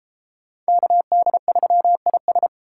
Morse code, heard as a keyed tone, about 29 wpm, 700 Hz. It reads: KD3IS